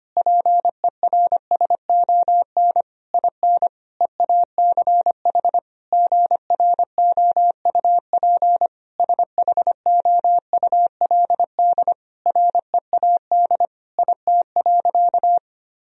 Morse code, heard as a keyed tone, about 25 wpm, 700 Hz. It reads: PERSON IN EACH GROUP SHOULD READ IT.
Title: Sentences from Top 1000 Words - Review Part 1 - 25wpm